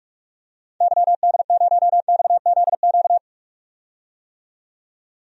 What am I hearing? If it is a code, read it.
Morse code, heard as a keyed tone, about 46 wpm, 700 Hz. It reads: YD0XZQ
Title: DXCC-100 Call Signs - 6 Characters - 45wpm